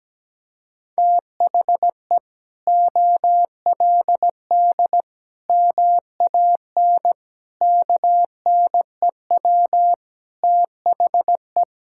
Morse code, heard as a keyed tone, about 17 wpm, 700 Hz. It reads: THE OLD MAN KNEW THE
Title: Sentences from Top 1000 Words - Review Part 6 - 17wpm